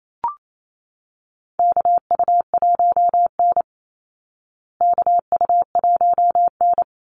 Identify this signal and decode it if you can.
Morse code, heard as a keyed tone, about 28 wpm, 700 Hz. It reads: KU1N KU1N
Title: Call Signs Repeated 3x - 28wpm